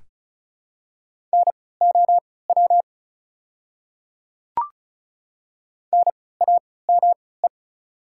Morse code, heard as a keyed tone, about 36 wpm, spaced as for 20 wpm, 700 Hz. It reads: NOW NAME
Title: 3x Character Spacing - Top 200 Words - 35wpm